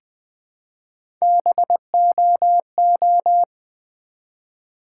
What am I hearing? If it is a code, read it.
Morse code, heard as a keyed tone, about 20 wpm, 700 Hz. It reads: BOO
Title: Build A Word - 1 Letter At A Time - 3 to 5-Letter Words - 20wpm